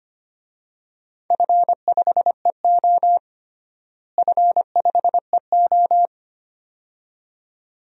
Morse code, heard as a keyed tone, about 25 wpm, 700 Hz. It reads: F5EO F5EO
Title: DXCC-100 Call Signs - 4 Characters - Repeated 2x - 25wpm